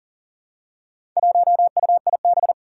Morse code, heard as a keyed tone, about 43 wpm, 700 Hz. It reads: 1UIB
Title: Four Challenging Letter-Number Combinations 40wpm